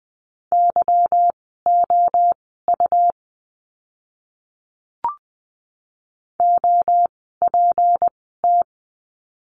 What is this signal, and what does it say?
Morse code, heard as a keyed tone, about 20 wpm, spaced as for 15 wpm, 700 Hz. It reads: YOU OPT
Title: 153 Lesson 25f - Letter p - Words 20wpm